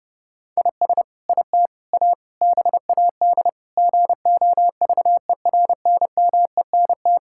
Morse code, heard as a keyed tone, about 30 wpm, 700 Hz. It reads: IS IT A BAD GOVERNMENT
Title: Sentences from Top 1000 Words - Review Part 5 - 30wpm